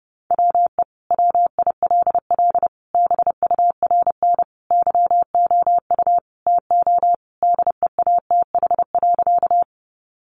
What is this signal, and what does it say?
Morse code, heard as a keyed tone, about 30 wpm, 700 Hz. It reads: WE WILL BURN YOU TO DEATH.
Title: Sentences from Top 1300 Words 30wpm